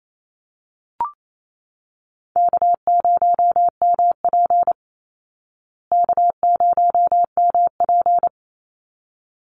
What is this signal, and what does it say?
Morse code, heard as a keyed tone, about 28 wpm, 700 Hz. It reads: K0MP K0MP
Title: Call Signs Repeated 3x - 28wpm